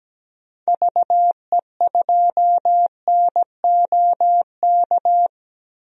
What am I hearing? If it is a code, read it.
Morse code, heard as a keyed tone, about 17 wpm, 700 Hz. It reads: VE2NOK
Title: DXCC-100 Call Signs - 6 Characters - 17wpm